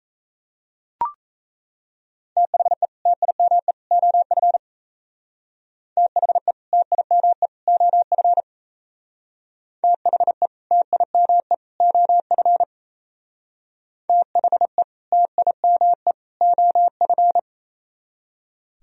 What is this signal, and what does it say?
Morse code, heard as a keyed tone, about 37 wpm, 700 Hz. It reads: THE TIME OF THE TIME OF THE TIME OF THE TIME OF